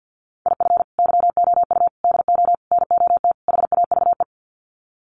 Morse code, heard as a keyed tone, about 50 wpm, 700 Hz. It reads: IF YOU DO NOT HAVE
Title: Sets of 5 Words 50wpm